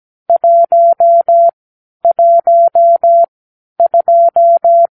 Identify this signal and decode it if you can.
Morse code, heard as a keyed tone, about 17 wpm, spaced as for 10 wpm, 680 Hz. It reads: 112
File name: k7qo / Lesson 46 (1-4)